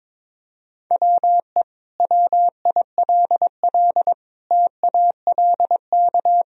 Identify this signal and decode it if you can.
Morse code, heard as a keyed tone, about 22 wpm, 700 Hz. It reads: WE WILL TALK
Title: Sentences from Top 1000 Words - Review Part 4 - 22wpm